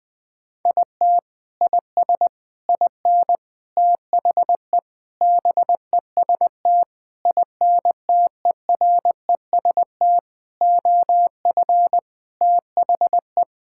Morse code, heard as a keyed tone, about 20 wpm, 700 Hz. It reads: IT IS IN THE BEST INTEREST OF THE
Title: Sentences with 6-Word Phrases - 20wpm